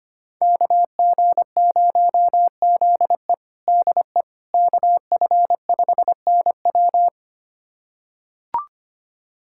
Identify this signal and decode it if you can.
Morse code, heard as a keyed tone, about 25 wpm, 700 Hz. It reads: KG0ZE DE KF5NW
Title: USA Callsign DE Callsign - 5 Characters - 25wpm